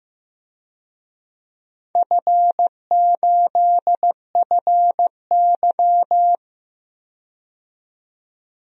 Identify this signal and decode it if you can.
Morse code, heard as a keyed tone, about 15 wpm, 700 Hz. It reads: F8FY
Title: DXCC-100 Call Signs - 4 Characters - 15wpm